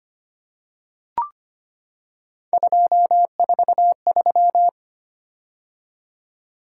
Morse code, heard as a keyed tone, about 25 wpm, 700 Hz. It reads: 243
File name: Three Letter-Number Combinations - 25wpm